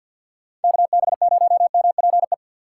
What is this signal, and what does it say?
Morse code, heard as a keyed tone, about 46 wpm, 700 Hz. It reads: KB0MPE
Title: USA Call Signs - 6 Characters - 3x Character Spacing - 50wpm